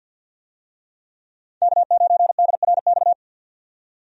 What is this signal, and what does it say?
Morse code, heard as a keyed tone, about 53 wpm, 700 Hz. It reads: K9DRX
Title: USA Call Signs - 5 Characters - 50wpm